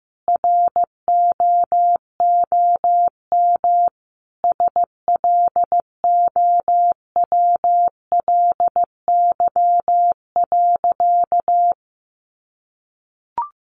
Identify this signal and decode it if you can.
Morse code, heard as a keyed tone, about 15 wpm, 700 Hz. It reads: ROOM SLOWLY.
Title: Sentences from Top 1100 Words 15wpm